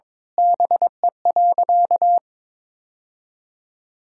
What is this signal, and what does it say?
Morse code, heard as a keyed tone, about 22 wpm, 700 Hz. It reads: BE.
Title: Sentences from Top 1000 Words - Review Part 5 - 22wpm